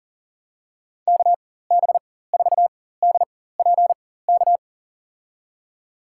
Morse code, heard as a keyed tone, about 40 wpm, spaced as for 19 wpm, 700 Hz. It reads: KB4DPK
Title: USA Call Signs - 4x Character Spacing - 40wpm